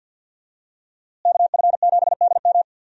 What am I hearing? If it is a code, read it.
Morse code, heard as a keyed tone, about 46 wpm, 700 Hz. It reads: KF7DM
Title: USA Call Signs - 5 Characters - Repeated 3x - 50wpm